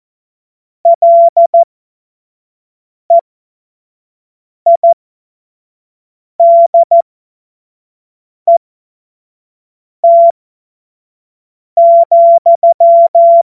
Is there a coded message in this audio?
Yes: Morse code from 0.9 to 13.4 s, heard as a keyed tone, about 14 wpm, spaced as for 5 wpm, 680 Hz. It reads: LEIDET,